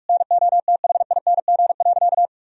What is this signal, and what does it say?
Morse code, heard as a keyed tone, about 46 wpm, 700 Hz. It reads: NOTHING.
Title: Sentences from Top 1000 Words - Review Part 3 - 45wpm